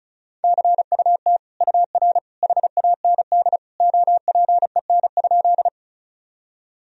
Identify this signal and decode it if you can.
Morse code, heard as a keyed tone, about 36 wpm, 700 Hz. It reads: CUT UR HAND OPEN?
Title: Sentences from Top 300 Words and Abbreviations 35wpm